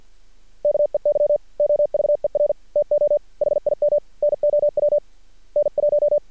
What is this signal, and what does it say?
Morse code, heard as a keyed tone, about 50 wpm, 590 Hz. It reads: KEY OVER TO HIM NOW N1
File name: archived / CRSnet